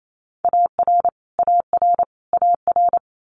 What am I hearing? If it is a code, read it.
Morse code, heard as a keyed tone, about 28 wpm, 700 Hz. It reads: AR AR AR
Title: US States Repeated 3x 28wpm